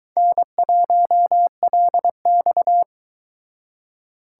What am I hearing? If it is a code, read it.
Morse code, heard as a keyed tone, about 22 wpm, 700 Hz. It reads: N1LX